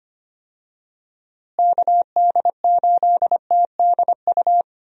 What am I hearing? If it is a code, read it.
Morse code, heard as a keyed tone, about 25 wpm, 700 Hz. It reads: KD8TDU
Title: USA Call Signs - 6 Characters - Repeated 3x - 25wpm